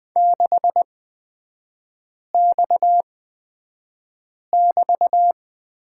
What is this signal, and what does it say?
Morse code, heard as a keyed tone, about 20 wpm, spaced as for 5 wpm, 690 Hz. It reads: 6X<BT>
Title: 229 Lesson 37e - Letter x - Three Characters 20wpm